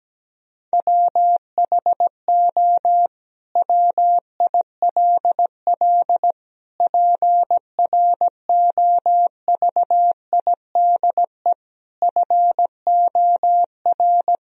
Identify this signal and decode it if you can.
Morse code, heard as a keyed tone, about 17 wpm, 700 Hz. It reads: WHO WILL PROVIDE FOR
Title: Sentences from Top 1000 Words - Review Part 3 - 17wpm